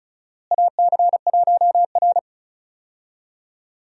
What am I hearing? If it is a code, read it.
Morse code, heard as a keyed tone, about 36 wpm, 700 Hz. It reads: AC1R
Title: USA Call Signs - 4 Characters - Repeated 3x - 35wpm